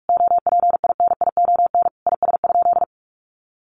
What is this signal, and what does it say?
Morse code, heard as a keyed tone, about 44 wpm, 700 Hz. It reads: OPINION IS?
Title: Sentences from Top 1200 Words 45wpm